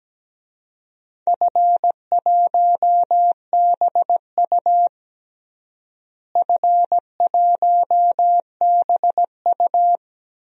Morse code, heard as a keyed tone, about 17 wpm, 700 Hz. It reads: F1BU F1BU
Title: DXCC-100 Call Signs - 4 Characters - Repeated 3x - 17wpm